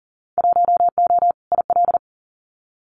Morse code, heard as a keyed tone, about 38 wpm, 700 Hz. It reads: 1O IL
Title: Winter Field Day Exchanges 40wpm